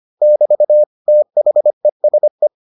Morse code, heard as a keyed tone, about 25 wpm, 600 Hz. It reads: <BT> THESE